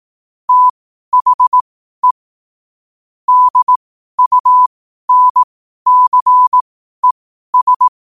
Morse code, heard as a keyed tone, about 18 wpm, spaced as for 12 wpm, 1000 Hz. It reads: THE DUNCES
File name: qotdfiles / 12wpm